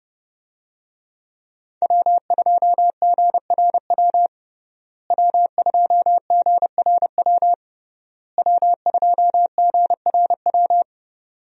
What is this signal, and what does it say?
Morse code, heard as a keyed tone, about 30 wpm, 700 Hz. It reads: W2GRW W2GRW W2GRW
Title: USA Call Signs - 5 Characters - Repeated 3x - 30wpm